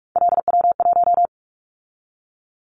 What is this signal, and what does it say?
Morse code, heard as a keyed tone, about 46 wpm, 700 Hz. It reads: LW1
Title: Callsign Prefixes for Top 100 DXCC - 45wpm